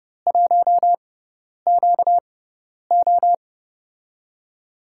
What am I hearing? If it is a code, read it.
Morse code, heard as a keyed tone, about 30 wpm, spaced as for 10 wpm, 700 Hz. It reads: 1QO
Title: USA Call Signs - 4 Characters - 6x Character Spacing - 30wpm